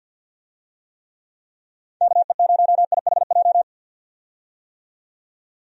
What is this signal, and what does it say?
Morse code, heard as a keyed tone, about 48 wpm, 700 Hz. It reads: KE0IHJ
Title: USA Call Signs - 6 Characters - 50wpm